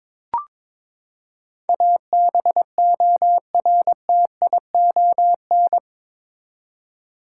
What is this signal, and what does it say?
Morse code, heard as a keyed tone, about 22 wpm, 700 Hz. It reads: ABORTION
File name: practice / 22wpm / Eight Letter Words - 22wpm